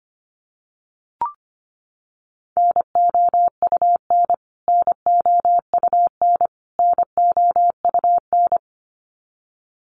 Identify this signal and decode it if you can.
Morse code, heard as a keyed tone, about 25 wpm, 700 Hz. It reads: NOUN NOUN NOUN
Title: Top 500 Words Repeated 25wpm